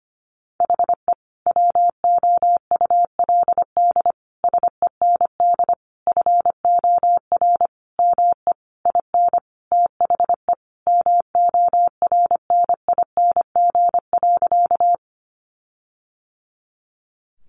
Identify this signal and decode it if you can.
Morse code, heard as a keyed tone, about 25 wpm, 700 Hz. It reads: HE WOULD SEND FOR ME IN THE MORNING.
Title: Sentences from Top 1500 Words 25wpm